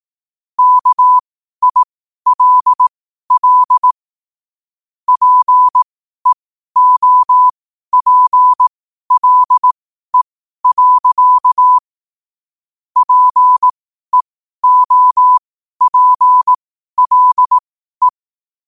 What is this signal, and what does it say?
Morse code, heard as a keyed tone, about 18 wpm, spaced as for 12 wpm, 1000 Hz. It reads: KILL PEOPLE. PEOPLE
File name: qotdfiles / 12wpm